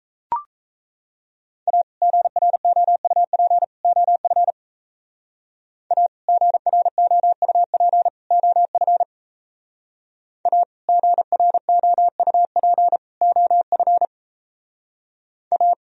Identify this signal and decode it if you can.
Morse code, heard as a keyed tone, about 34 wpm, 700 Hz. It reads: A GROUP OF A GROUP OF A GROUP OF A
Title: Speed-Racing - Sets of 3 Words 28wpm